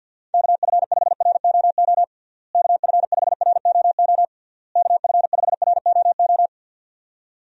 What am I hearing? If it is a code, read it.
Morse code, heard as a keyed tone, about 52 wpm, 700 Hz. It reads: KF5ROO KF5ROO KF5ROO
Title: USA Call Signs - 6 Characters - Repeated 3x - 50wpm